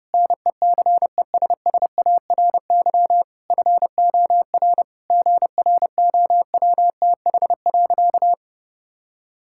Morse code, heard as a keyed tone, about 30 wpm, 700 Hz. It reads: NECESSARY FOR GROWTH.
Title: Sentences from Top 1000 Words - Review Part 5 - 30wpm